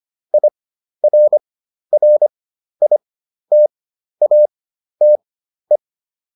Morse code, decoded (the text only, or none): IRRITATE